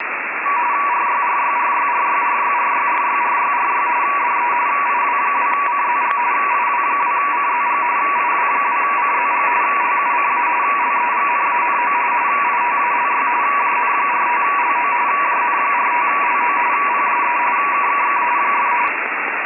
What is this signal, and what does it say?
RTTY, about 45.45 baud, with radio noise. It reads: RYRYRYRYRYRYRYRYRY DE IW2CZG IW2CZG IW2CZG LUCA LEGNANO ITALY JN45LO JN45LO PSE KKK !
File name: Modulazioni / RTTY a 144.600 MHz